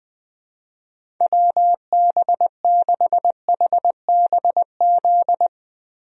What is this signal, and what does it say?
Morse code, heard as a keyed tone, about 21 wpm, 700 Hz. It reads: WB6HBZ
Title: USA Call Signs - 6 Characters - 20wpm